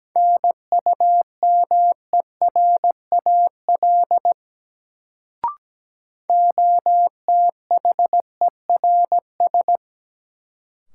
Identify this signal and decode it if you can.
Morse code, heard as a keyed tone, about 17 wpm, 700 Hz. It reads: NUMERAL OTHERS
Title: Sentences from Top 1000 Words - Review Part 2 - New Word Drill - 17wpm